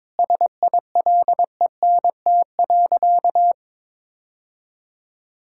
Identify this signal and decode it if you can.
Morse code, heard as a keyed tone, about 22 wpm, 700 Hz. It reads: SILENT.
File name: practice / 22wpm / Sentences with 4-Word Phrases - 22wpm